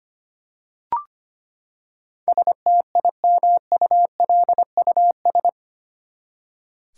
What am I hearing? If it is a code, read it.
Morse code, heard as a keyed tone, about 25 wpm, 700 Hz. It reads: STIMULUS
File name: Eight Letter Words - 25wpm